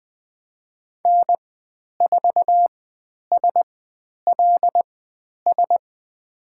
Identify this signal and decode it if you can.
Morse code, heard as a keyed tone, about 20 wpm, spaced as for 10 wpm, 700 Hz. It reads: N4SLS
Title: 095 Lesson 16g - Letter c - Call Signs 20wpm